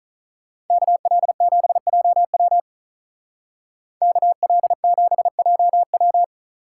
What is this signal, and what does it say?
Morse code, heard as a keyed tone, about 43 wpm, 700 Hz. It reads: KL7JW KL7JW
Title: Speed Racing - USA 2x2 Call Signs 35wpm